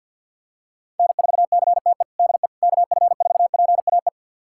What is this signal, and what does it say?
Morse code, heard as a keyed tone, about 46 wpm, 700 Hz. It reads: N4CTE DE KF4PAE
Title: USA Callsign DE Callsign 50wpm